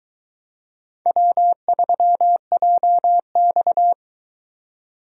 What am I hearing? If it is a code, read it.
Morse code, heard as a keyed tone, about 24 wpm, 700 Hz. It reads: W3JX